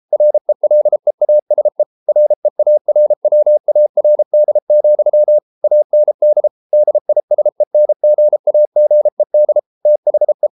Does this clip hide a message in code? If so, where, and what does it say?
Yes: Morse code from 0.1 to 10.5 s, heard as a keyed tone, about 33 wpm, 600 Hz. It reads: RELEASE REARWARD, AND DISENGAGED THE